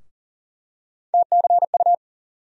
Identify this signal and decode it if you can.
Morse code, heard as a keyed tone, about 40 wpm, 700 Hz. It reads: TCU
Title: USA Call Sign Postfixes - 3 Letters 40wpm